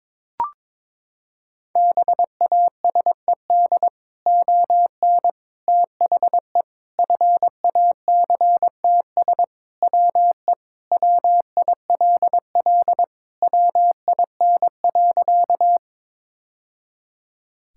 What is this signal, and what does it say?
Morse code, heard as a keyed tone, about 22 wpm, 700 Hz. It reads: BASED ON THE FACTS WE WILL WIN.
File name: practice / 22wpm / Sentences with 3-Word Phrases - 22wpm